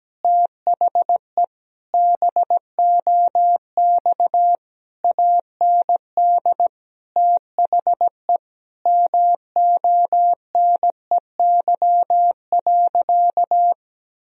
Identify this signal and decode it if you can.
Morse code, heard as a keyed tone, about 17 wpm, 700 Hz. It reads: THE BOX AND THE MONEY.
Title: Sentences from Top 1000 Words - Review Part 2 - 17wpm